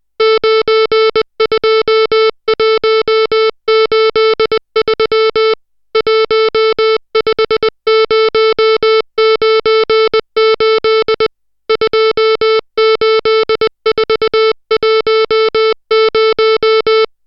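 Morse code, heard as a keyed tone, about 20 wpm, 440 Hz, with noise. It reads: 92183 15098 28410
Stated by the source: Morse code drill of digits at 60 characters per minute.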